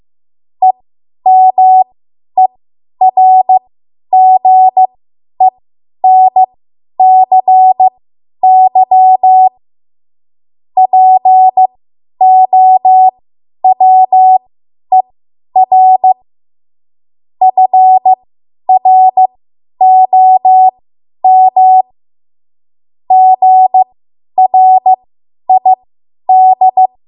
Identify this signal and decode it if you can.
Morse code, heard as a keyed tone, about 15 wpm, spaced as for 10 wpm, 750 Hz. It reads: EMERGENCY POWER FROM GRID
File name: Archive / 10 WPM / 10WPM